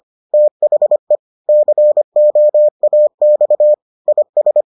Morse code, heard as a keyed tone, about 25 wpm, 600 Hz. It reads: THE COAX IS